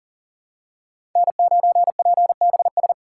Morse code, heard as a keyed tone, about 40 wpm, 700 Hz. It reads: N9PBS
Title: USA Call Signs - 5 Characters - 2x Character Spacing - 40wpm